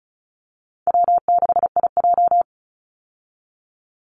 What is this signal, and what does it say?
Morse code, heard as a keyed tone, about 34 wpm, 700 Hz. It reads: W6IJ
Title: Call Signs Repeated 2x - 35wpm